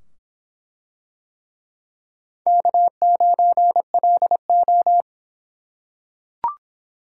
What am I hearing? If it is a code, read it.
Morse code, heard as a keyed tone, about 23 wpm, 700 Hz. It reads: K9LO